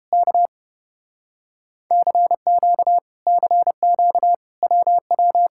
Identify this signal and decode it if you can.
Morse code, heard as a keyed tone, about 30 wpm, 700 Hz. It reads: K CQ CQ WW